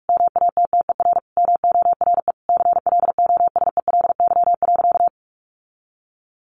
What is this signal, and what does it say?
Morse code, heard as a keyed tone, about 45 wpm, 700 Hz. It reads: MATTER MORE CLOSELY.
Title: Sentences from Top 1400 Words 45wpm